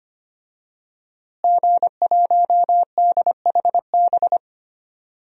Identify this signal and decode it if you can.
Morse code, heard as a keyed tone, about 25 wpm, 700 Hz. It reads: G1DHB
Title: DXCC-100 Call Signs - 5 Characters - Repeated 3x - 25wpm